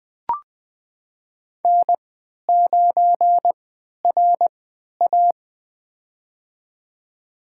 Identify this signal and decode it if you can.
Morse code, heard as a keyed tone, about 21 wpm, spaced as for 12 wpm, 700 Hz. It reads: N9RA
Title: USA Call Signs - 4 Characters - 3x Character Spacing - 20wpm